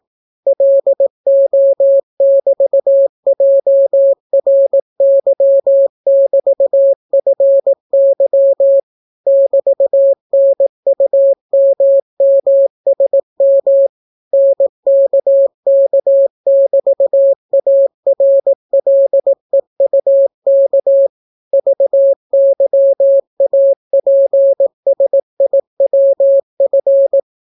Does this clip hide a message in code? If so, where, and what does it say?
Yes: Morse code from 0.5 to 27.2 s, heard as a keyed tone, about 18 wpm, 550 Hz. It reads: LO<BT>JRY<BT>FY <BT>NUMMSM NKK<BT>ARLEUK VYAPSIWF